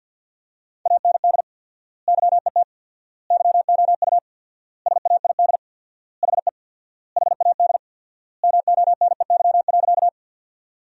Morse code, heard as a keyed tone, about 53 wpm, 700 Hz. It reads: AND YET YOU SAID HE HAD MONEY.
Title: 4x Word-Spacing - Sentences from Top 500 Words - 50wpm